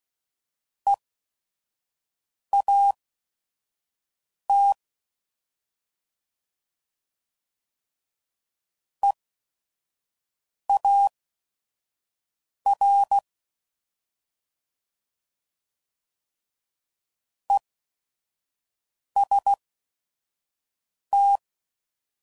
Morse code, heard as a keyed tone, about 16 wpm, spaced as for 5 wpm, 800 Hz. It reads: EAT EAR EST
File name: Prefixes / By First Letter / Es